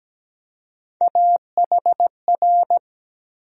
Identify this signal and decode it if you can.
Morse code, heard as a keyed tone, about 17 wpm, 700 Hz. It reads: AHR
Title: USA Call Sign Postfixes - 3 Letters 17wpm